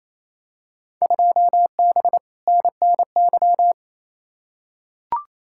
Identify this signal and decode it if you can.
Morse code, heard as a keyed tone, about 28 wpm, 700 Hz. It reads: 2B NNY
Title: Field Day Exchanges Encore - 28wpm